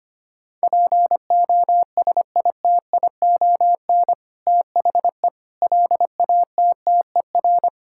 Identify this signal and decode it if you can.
Morse code, heard as a keyed tone, about 25 wpm, 700 Hz. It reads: POSITION THE LATTER
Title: Sentences from Top 1600 Words 25wpm